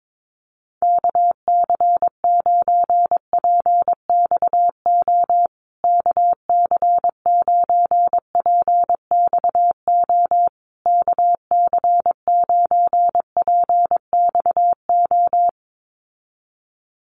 Morse code, heard as a keyed tone, about 22 wpm, 700 Hz. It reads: KC9PXO KC9PXO KC9PXO
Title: USA Call Signs Repeated 3x 22wpm